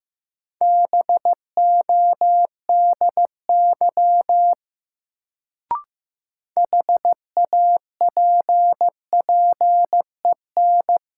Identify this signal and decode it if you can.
Morse code, heard as a keyed tone, about 15 wpm, 700 Hz. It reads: BODY HAPPEN
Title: Sentences from Top 400 Words - New Word Drill - 15wpm